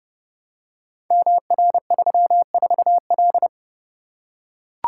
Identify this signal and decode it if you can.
Morse code, heard as a keyed tone, about 30 wpm, 700 Hz. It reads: MR34L
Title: Code Groups - 5 Letter:Numbers - 30wpm